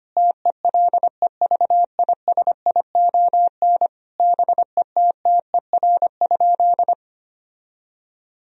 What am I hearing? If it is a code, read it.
Morse code, heard as a keyed tone, about 25 wpm, 700 Hz. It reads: TELEVISION BETTER?
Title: Sentences from Top 1000 Words - Review Part 4 - 25wpm